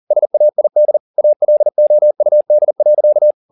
Morse code, heard as a keyed tone, about 40 wpm, 600 Hz. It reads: SAID ALOUD.